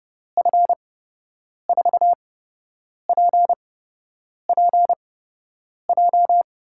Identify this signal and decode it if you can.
Morse code, heard as a keyed tone, about 32 wpm, spaced as for 8 wpm, 700 Hz. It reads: F4PPJ
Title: USA Call Signs - 6 Characters - 8x Character Spacing - 30wpm